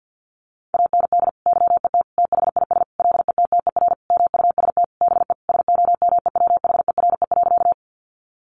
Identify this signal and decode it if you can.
Morse code, heard as a keyed tone, about 50 wpm, 700 Hz. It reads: AND YET THIS LETTER MUST BE SOMEWHERE.
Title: Sentences from Top 1400 Words 50wpm